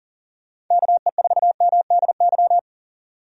Morse code, heard as a keyed tone, about 40 wpm, 700 Hz. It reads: KE4MDY
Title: USA Call Signs - 6 Characters - 2x Character Spacing - 40wpm